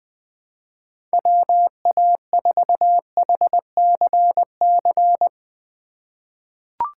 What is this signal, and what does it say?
Morse code, heard as a keyed tone, about 20 wpm, 700 Hz. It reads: WA4HCC
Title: K1USN SST QSO Practice - 20wpm